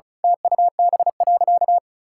Morse code, heard as a keyed tone, about 36 wpm, 700 Hz. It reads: TUB.
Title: Pangram Challenge v2 - 35wpm